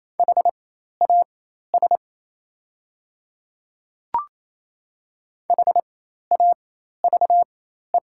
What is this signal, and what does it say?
Morse code, heard as a keyed tone, about 27 wpm, spaced as for 13 wpm, 690 Hz. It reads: HAS HAVE
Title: 4x Character Spacing - Top 200 Words - 28wpm